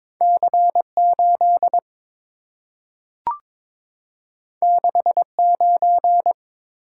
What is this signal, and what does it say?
Morse code, heard as a keyed tone, about 23 wpm, 690 Hz. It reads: C8 69
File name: practice / 22wpm / Two Letter-Number Combinations - 22wpm